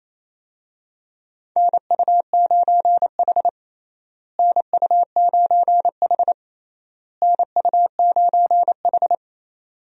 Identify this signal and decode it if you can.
Morse code, heard as a keyed tone, about 28 wpm, 700 Hz. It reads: NU9H NU9H NU9H
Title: USA Call Signs - 4 Characters - Repeated 3x - 28wpm